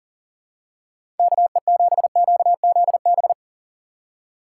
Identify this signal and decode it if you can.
Morse code, heard as a keyed tone, about 39 wpm, 700 Hz. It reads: KE7QZB
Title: USA Call Signs - 6 Characters - 6x Character Spacing - 40wpm